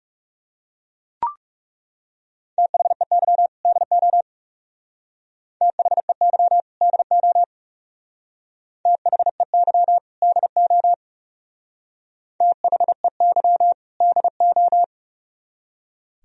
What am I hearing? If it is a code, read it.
Morse code, heard as a keyed tone, about 36 wpm, 700 Hz. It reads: THEY DO THEY DO THEY DO THEY DO
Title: Speed-Racing - Sets of 2 Words 30wpm